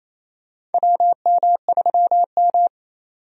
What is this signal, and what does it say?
Morse code, heard as a keyed tone, about 28 wpm, 690 Hz. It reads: WM3M
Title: USA Call Signs - 4 Characters - 28wpm